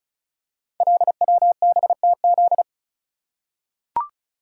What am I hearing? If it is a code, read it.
Morse code, heard as a keyed tone, about 36 wpm, 700 Hz. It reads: LWBTZ
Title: Code Groups - 5 Letter:Numbers - 35wpm